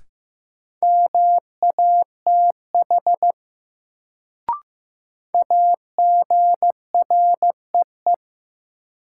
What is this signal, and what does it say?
Morse code, heard as a keyed tone, about 15 wpm, 700 Hz. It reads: MATH AGREE
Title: Sentences from Top 1000 Words - Review Part 6 - New Word Drill - 15wpm